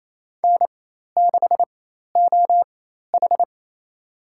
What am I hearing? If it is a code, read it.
Morse code, heard as a keyed tone, about 27 wpm, spaced as for 13 wpm, 700 Hz. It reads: N6OH
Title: USA Call Signs - 4 Characters - 4x Character Spacing - 28wpm